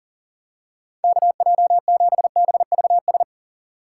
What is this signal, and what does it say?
Morse code, heard as a keyed tone, about 40 wpm, 700 Hz. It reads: KJ7BVS
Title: USA Call Signs - 2x Character Spacing - 40wpm